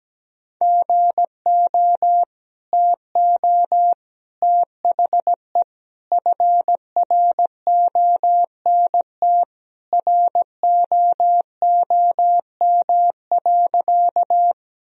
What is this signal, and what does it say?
Morse code, heard as a keyed tone, about 17 wpm, 700 Hz. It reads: GO TO THE FRONT ROOM.
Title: Sentences from Top 1000 Words - Review Part 2 - 17wpm